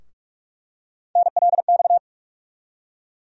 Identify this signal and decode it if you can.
Morse code, heard as a keyed tone, about 46 wpm, 700 Hz. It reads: NLX
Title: USA Call Sign Postfixes - 3 Letters 45wpm